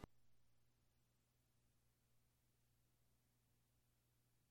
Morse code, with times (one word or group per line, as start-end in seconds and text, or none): none